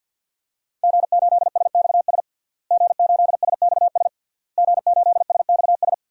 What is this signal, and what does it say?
Morse code, heard as a keyed tone, about 50 wpm, 700 Hz. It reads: G8SXS G8SXS G8SXS
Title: DXCC-100 Call Signs - 5 Characters - Repeated 3x - 50wpm